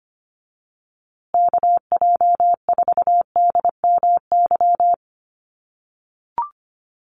Morse code, heard as a keyed tone, about 25 wpm, 700 Hz. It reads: KJ4DMY
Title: USA Call Signs Repeated 2x 25wpm